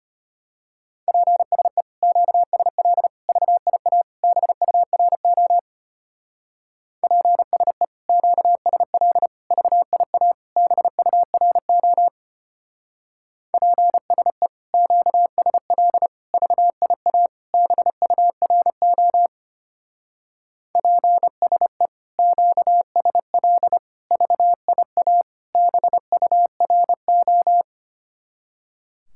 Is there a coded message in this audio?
Yes: Morse code from 1.1 to 27.6 s, heard as a keyed tone, about 32 wpm, 700 Hz. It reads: PSE QSL VIA BURO PSE QSL VIA BURO PSE QSL VIA BURO PSE QSL VIA BURO